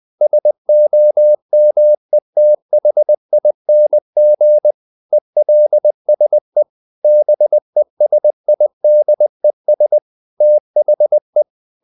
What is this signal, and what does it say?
Morse code, heard as a keyed tone, about 20 wpm, 600 Hz. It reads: SOMETHING ELSE BESIDES THE
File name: the old man and the sea part 2 20wpm